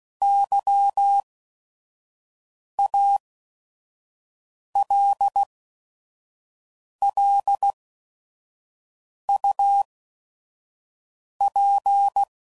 Morse code, heard as a keyed tone, about 16 wpm, spaced as for 5 wpm, 800 Hz. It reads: YALLUP
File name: CW Code Groups @5WPM / QSOs / kc7haa